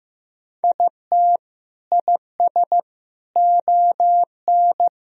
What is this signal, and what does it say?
Morse code, heard as a keyed tone, about 15 wpm, 700 Hz. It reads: IT IS ON
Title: Sentences from Top 1000 Words - Review Part 2 - 15wpm